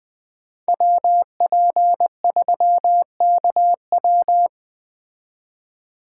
Morse code, heard as a keyed tone, about 20 wpm, 700 Hz. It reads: WP3KW
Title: Speed Racing - USA 2x2 Call Signs 17wpm